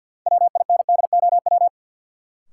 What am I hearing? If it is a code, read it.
Morse code, heard as a keyed tone, about 52 wpm, 700 Hz. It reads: WINDOW
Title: Sentences from Top 1600 Words - New Word Drill Alt - 50wpm